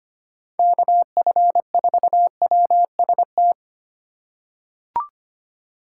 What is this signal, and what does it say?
Morse code, heard as a keyed tone, about 25 wpm, 700 Hz. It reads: KF4WST
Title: USA Call Signs - 3x Character Spacing - 25wpm